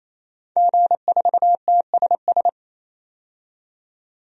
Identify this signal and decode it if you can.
Morse code, heard as a keyed tone, about 28 wpm, 700 Hz. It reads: G4TSS
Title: DXCC-100 Call Signs - 5 Characters - 28wpm